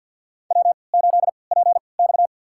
Morse code, heard as a keyed tone, about 46 wpm, spaced as for 27 wpm, 700 Hz. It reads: W8PX
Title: USA Call Signs - 4 Characters - 3x Character Spacing - 50wpm